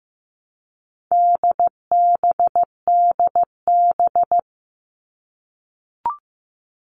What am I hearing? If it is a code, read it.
Morse code, heard as a keyed tone, about 15 wpm, 700 Hz. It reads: DBDB
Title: All Cut Numbers - 4-Digit 15wpm